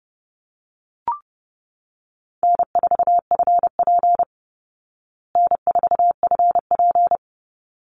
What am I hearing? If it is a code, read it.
Morse code, heard as a keyed tone, about 28 wpm, 700 Hz. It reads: N4FP N4FP
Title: Easy Call Signs Repeated 2x - 30wpm